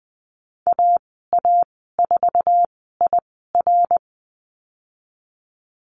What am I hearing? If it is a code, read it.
Morse code, heard as a keyed tone, about 20 wpm, spaced as for 15 wpm, 700 Hz. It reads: AA4IR
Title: 055 Lesson 10g - Letter r - Call Signs 20wpm